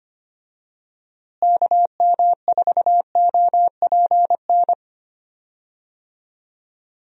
Morse code, heard as a keyed tone, about 25 wpm, 700 Hz. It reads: KM4OPN
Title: USA Call Signs - 6 Characters - Repeated 2x - 25wpm